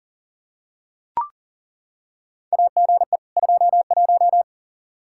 Morse code, heard as a keyed tone, about 40 wpm, 700 Hz. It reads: AGE 21
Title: Practice QSO - Exchange 3 - More Info - 40wpm